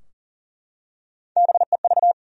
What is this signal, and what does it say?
Morse code, heard as a keyed tone, about 40 wpm, 700 Hz. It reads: BEV
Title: USA Common Names - Max 3 Letters - 40wpm